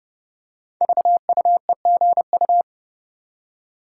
Morse code, heard as a keyed tone, about 30 wpm, 700 Hz. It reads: VUEGU
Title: Code Groups - 5 Letters - 30wpm